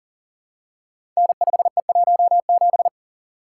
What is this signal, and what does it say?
Morse code, heard as a keyed tone, about 42 wpm, 700 Hz. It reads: N5E17
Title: Code Groups - 5 Letter:Numbers - 40wpm